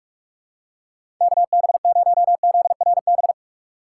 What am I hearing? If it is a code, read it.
Morse code, heard as a keyed tone, about 47 wpm, 700 Hz. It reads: KB0ZRB
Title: USA Call Signs - 6 Characters - 4x Character Spacing - 45wpm